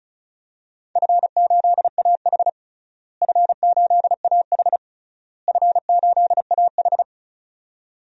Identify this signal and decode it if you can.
Morse code, heard as a keyed tone, about 36 wpm, 700 Hz. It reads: F8AH F8AH F8AH
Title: DXCC-100 Call Signs - 4 Characters - Repeated 3x - 35wpm